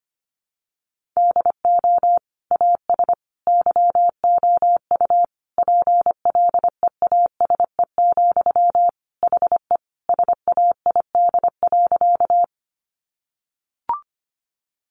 Morse code, heard as a keyed tone, about 25 wpm, 700 Hz. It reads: DO AS YOU PLEASE, HE SAID.
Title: Sentences from Top 1500 Words 25wpm